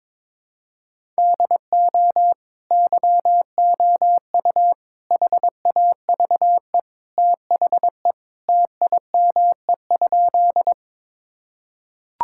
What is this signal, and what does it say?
Morse code, heard as a keyed tone, about 22 wpm, 700 Hz. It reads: DO YOU HAVE THE TIME?
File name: practice / 22wpm / Sentences with 2-Word Phrases - 22wpm